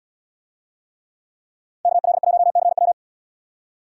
Morse code, heard as a keyed tone, about 62 wpm, 700 Hz. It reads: KL1CW
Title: Speed Racing - USA Call Signs 50wpm